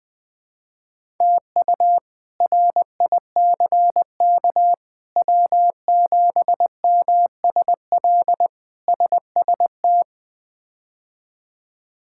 Morse code, heard as a keyed tone, about 20 wpm, 700 Hz. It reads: TU RICK W7MSL SST
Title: K1USN SST QSO Practice - 20wpm